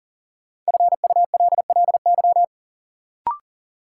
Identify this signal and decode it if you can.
Morse code, heard as a keyed tone, about 40 wpm, 700 Hz. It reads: FULLY
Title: Sentences from Top 1300 Words - New Word Drill Alt - 40wpm